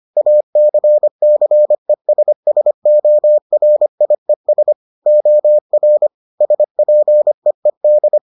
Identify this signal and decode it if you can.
Morse code, heard as a keyed tone, about 25 wpm, 600 Hz. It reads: ACCESSORIES OR SPEED